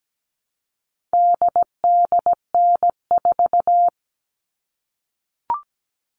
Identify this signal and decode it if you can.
Morse code, heard as a keyed tone, about 17 wpm, 700 Hz. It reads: DDN4
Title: All Cut Numbers - 4-Digit 17wpm